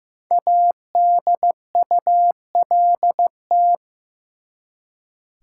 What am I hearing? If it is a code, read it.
Morse code, heard as a keyed tone, about 15 wpm, 700 Hz. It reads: ADULT
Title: Five Letter Words - 15wpm